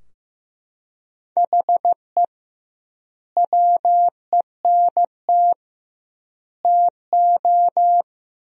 Morse code, heard as a keyed tone, about 15 wpm, 700 Hz. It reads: HE WENT TO
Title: 2x Word-Spacing - Sentences from Top 200 Words - 15wpm